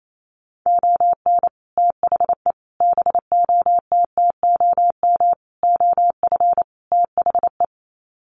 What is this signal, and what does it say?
Morse code, heard as a keyed tone, about 28 wpm, 700 Hz. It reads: ON THE BOTTOM OF THE
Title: Sets of 5 Words Encore2 28wpm